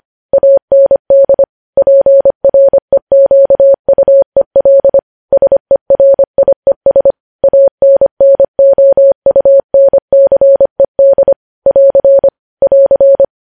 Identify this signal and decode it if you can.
Morse code, heard as a keyed tone, about 25 wpm, 550 Hz. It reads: AND PREQUEL SERIES ANNOUNCED <AR> <AR>
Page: news headlines in morse code at 25 wpm